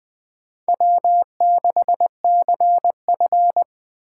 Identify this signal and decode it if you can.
Morse code, heard as a keyed tone, about 20 wpm, 700 Hz. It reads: W6CF
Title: USA Call Signs - 4 Characters - 3x Character Spacing - 20wpm